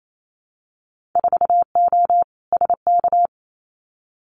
Morse code, heard as a keyed tone, about 28 wpm, 700 Hz. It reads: 4O SK
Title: Winter Field Day Exchanges Encore 28wpm